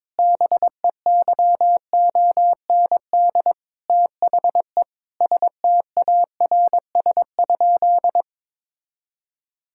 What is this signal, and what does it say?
Morse code, heard as a keyed tone, about 22 wpm, 700 Hz. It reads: BEYOND THE STARS?
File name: Difficult Sentences with Hints - 22wpm